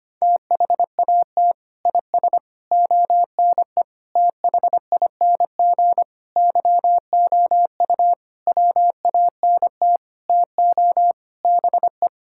Morse code, heard as a keyed tone, about 25 wpm, 700 Hz. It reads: THAT IS ONE THING YOU WANT TO BE
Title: Sentences from Top 1600 Words 25wpm